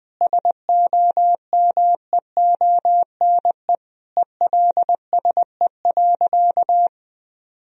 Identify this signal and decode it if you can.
Morse code, heard as a keyed tone, about 20 wpm, 700 Hz. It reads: SOMEONE ELSE.
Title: Sentences from Top 1000 Words - Review Part 4 - 20wpm